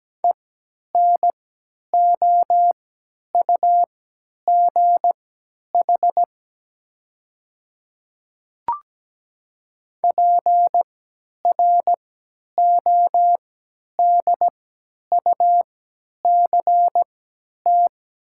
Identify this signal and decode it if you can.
Morse code, heard as a keyed tone, about 17 wpm, spaced as for 10 wpm, 700 Hz. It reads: ENOUGH PRODUCT
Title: 3x Character Spacing - Top 500 Words - 17wpm